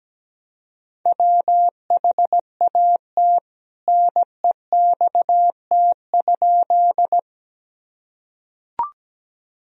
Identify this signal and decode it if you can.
Morse code, heard as a keyed tone, about 17 wpm, 700 Hz. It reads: WHAT NEXT?
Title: Sentences from Top 1000 Words - Review Part 1 - 17wpm